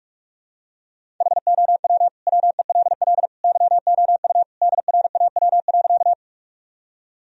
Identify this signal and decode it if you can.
Morse code, heard as a keyed tone, about 46 wpm, 700 Hz. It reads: HOW WELL YOU DRAW.
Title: Sentences from Top 1000 Words - Review Part 1 - 45wpm